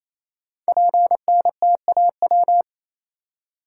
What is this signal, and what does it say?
Morse code, heard as a keyed tone, about 27 wpm, 700 Hz. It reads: PNTAW
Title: Code Groups - 5 Letters - 28wpm